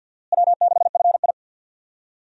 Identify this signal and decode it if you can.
Morse code, heard as a keyed tone, about 52 wpm, 700 Hz. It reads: W6FI
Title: USA Call Signs - 4 Characters - Repeated 3x - 50wpm